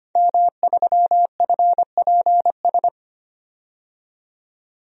Morse code, heard as a keyed tone, about 25 wpm, 700 Hz. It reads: M3FPS
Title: DXCC-100 Call Signs - 5 Characters - 25wpm